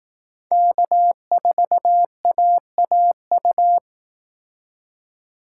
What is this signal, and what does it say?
Morse code, heard as a keyed tone, about 18 wpm, 700 Hz. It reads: K4AAU
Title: Speed Racing - USA 1x3 Call Signs 15wpm